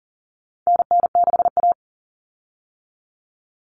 Morse code, heard as a keyed tone, about 40 wpm, 700 Hz. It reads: NN6A
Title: Common Cut Numbers - 4-Digit 40wpm